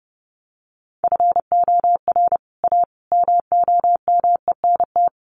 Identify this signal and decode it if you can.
Morse code, heard as a keyed tone, about 29 wpm, 700 Hz. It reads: FOR A MOMENT
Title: Sets of 3 Words 30wpm